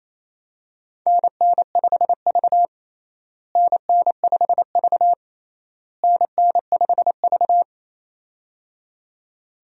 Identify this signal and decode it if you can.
Morse code, heard as a keyed tone, about 28 wpm, 700 Hz. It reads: NN5V NN5V NN5V
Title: USA Call Signs - 4 Characters - Repeated 3x - 28wpm